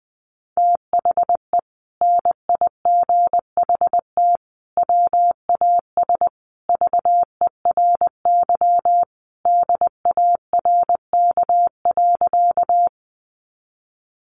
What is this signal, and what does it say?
Morse code, heard as a keyed tone, about 20 wpm, 700 Hz. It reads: THE NIGHT WAS VERY DARK.
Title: Sentences from Top 500 Words 20wpm